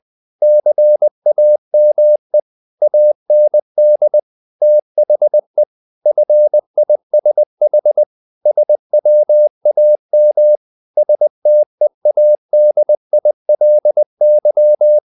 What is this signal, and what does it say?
Morse code, heard as a keyed tone, about 20 wpm, 600 Hz. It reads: CAME AND THE FISH SWAM STEADILY
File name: the old man and the sea / the old man and the sea part 1 20wpm